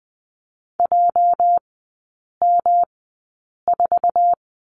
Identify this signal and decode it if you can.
Morse code, heard as a keyed tone, about 20 wpm, spaced as for 8 wpm, 700 Hz. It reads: JM4
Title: 205 Lesson 33e - Letter j - Three Characters 20wpm